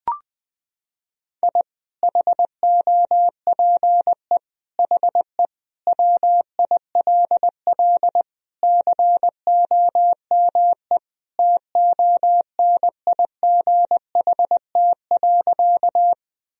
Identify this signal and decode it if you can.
Morse code, heard as a keyed tone, about 20 wpm, 700 Hz. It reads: I HOPE HE WILL COME TONIGHT.
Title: Sentences from Top 1000 Words - Review Part 3 - 20wpm